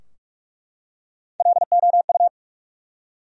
Morse code, heard as a keyed tone, about 43 wpm, 700 Hz. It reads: LOU
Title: USA Common Names - Max 5 Letters - 45wpm